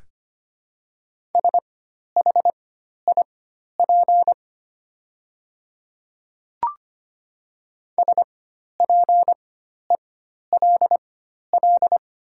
Morse code, heard as a keyed tone, about 25 wpm, spaced as for 12 wpm, 700 Hz. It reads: SHIP SPELL
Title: 4x Character Spacing - Top 500 Words - 25wpm